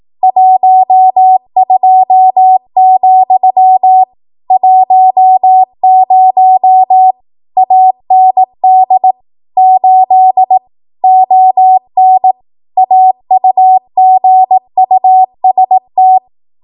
Morse code, heard as a keyed tone, about 18 wpm, 750 Hz. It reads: 12, 10 AND 8 ON AUGUST
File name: Archive / 18 WPM / 18WPM